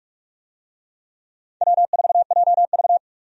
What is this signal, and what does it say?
Morse code, heard as a keyed tone, about 45 wpm, 700 Hz. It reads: W4JV
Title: USA Call Signs - 4 Characters - 45wpm